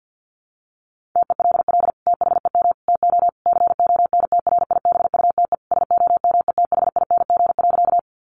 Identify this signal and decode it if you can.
Morse code, heard as a keyed tone, about 50 wpm, 700 Hz. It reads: TELL THEM TO CONTRIBUTE SOMETHING.
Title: Sentences from Top 1400 Words 50wpm